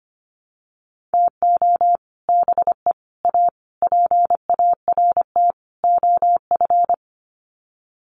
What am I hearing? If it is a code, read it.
Morse code, heard as a keyed tone, about 25 wpm, 700 Hz. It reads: TO BE A PART OF
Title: Sets of 5 Words 25wpm